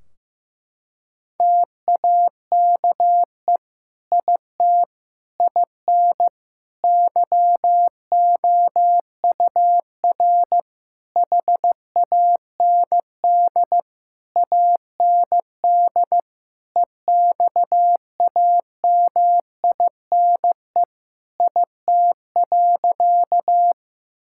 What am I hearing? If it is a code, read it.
Morse code, heard as a keyed tone, about 15 wpm, 700 Hz. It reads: TAKE IT IN YOUR HAND AND EXAMINE IT.
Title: Sentences from Top 1600 Words 15wpm